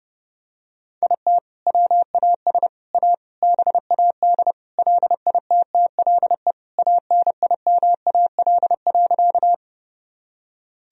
Morse code, heard as a keyed tone, about 30 wpm, 700 Hz. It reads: IT WAS A BAD LITTLE ANIMAL.
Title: Sentences from Top 1000 Words - Review Part 1 - 30wpm